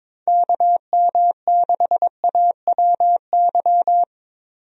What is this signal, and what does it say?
Morse code, heard as a keyed tone, about 22 wpm, 700 Hz. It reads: KM6AWY
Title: USA Call Signs - 6 Characters - 22wpm